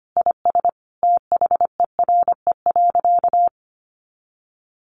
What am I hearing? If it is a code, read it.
Morse code, heard as a keyed tone, about 25 wpm, 700 Hz. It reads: IS THERE.
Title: Sentences from Top 1000 Words 25wpm